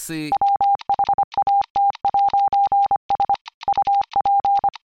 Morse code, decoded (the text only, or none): WHAT'S UP